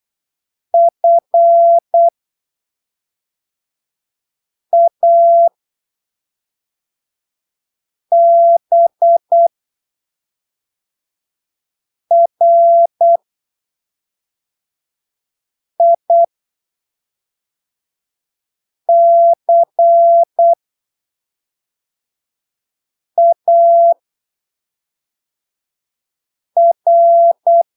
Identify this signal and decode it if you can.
Morse code, heard as a keyed tone, about 8 wpm, spaced as for 3 wpm, 680 Hz. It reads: FABRICAR
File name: arquivos / texto / texto 20 08ppm